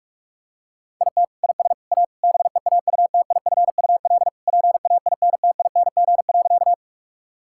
Morse code, heard as a keyed tone, about 46 wpm, 700 Hz. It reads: IT IS A BEAUTIFUL PAINTING.
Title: Sentences from Top 1000 Words - Review Part 6 - 45wpm